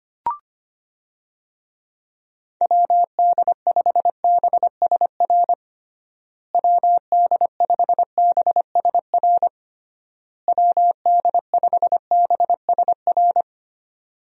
Morse code, heard as a keyed tone, about 25 wpm, 700 Hz. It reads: WD5BSR WD5BSR WD5BSR
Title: USA Call Signs - 6 Characters - Repeated 3x - 25wpm